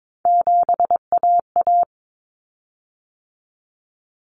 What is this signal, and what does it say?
Morse code, heard as a keyed tone, about 22 wpm, 700 Hz. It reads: 7AA
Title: Common Cut Numbers - 3-Digit 22wpm